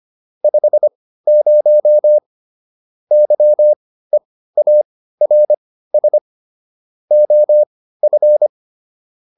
Morse code, heard as a keyed tone, about 25 wpm, spaced as for 15 wpm, 600 Hz. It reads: 50 YEARS OF